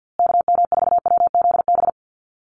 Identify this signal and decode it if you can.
Morse code, heard as a keyed tone, about 50 wpm, 700 Hz. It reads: KM4WZB
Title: USA Call Signs Repeated 3x 50wpm